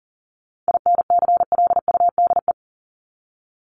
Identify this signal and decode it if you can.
Morse code, heard as a keyed tone, about 38 wpm, 700 Hz. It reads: INCLUDE
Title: ED Suffix - Top 100 Words with Base Words 40wpm